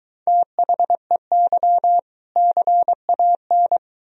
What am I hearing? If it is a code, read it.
Morse code, heard as a keyed tone, about 23 wpm, 700 Hz. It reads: THEY CAN